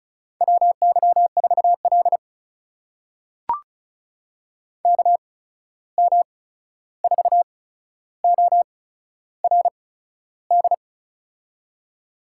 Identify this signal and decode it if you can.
Morse code, heard as a keyed tone, about 36 wpm, spaced as for 13 wpm, 700 Hz. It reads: WY4L KM4ORD
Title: USA Call Signs - 8x Character Spacing - 35wpm